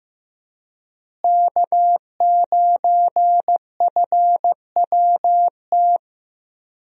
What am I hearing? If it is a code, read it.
Morse code, heard as a keyed tone, about 15 wpm, 700 Hz. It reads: K9FWT
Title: USA Call Signs - 5 Characters - Repeated 3x - 15wpm